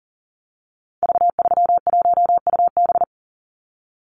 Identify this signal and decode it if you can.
Morse code, heard as a keyed tone, about 40 wpm, 700 Hz. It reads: V31UB
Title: Call Signs - 40wpm